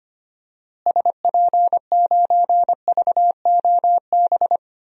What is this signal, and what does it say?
Morse code, heard as a keyed tone, about 25 wpm, 700 Hz. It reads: SP9VOB
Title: DXCC-100 Call Signs - 6 Characters - Repeated 2x - 25wpm